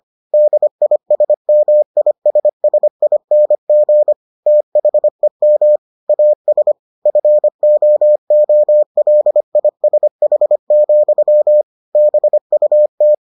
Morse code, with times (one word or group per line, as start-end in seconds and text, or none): 0.3-4.1 DISMISSING
4.5-5.8 THEM
6.1-6.7 AS
7.1-11.6 FOOLISH,
11.9-13.1 BUT